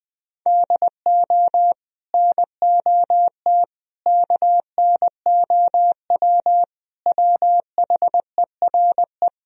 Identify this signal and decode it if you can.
Morse code, heard as a keyed tone, about 20 wpm, 700 Hz. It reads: DO NOT KNOW WHERE